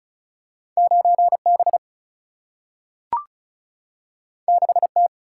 Morse code, heard as a keyed tone, about 36 wpm, 700 Hz. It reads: 9 B 6 T
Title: Single-Letter Single-Number 35wpm